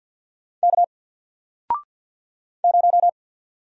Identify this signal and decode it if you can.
Morse code, heard as a keyed tone, about 47 wpm, 700 Hz. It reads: K 0
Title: Single Letter-Number Alt 50wpm